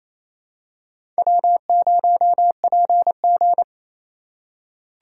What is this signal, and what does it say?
Morse code, heard as a keyed tone, about 28 wpm, 700 Hz. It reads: W0PG
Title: USA Call Signs - 2x Character Spacing - 28wpm